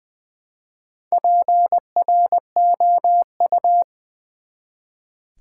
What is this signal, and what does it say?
Morse code, heard as a keyed tone, about 20 wpm, 700 Hz. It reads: PROU
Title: Build A Word - 1 Letter At A Time - 5-Letter Words - 20wpm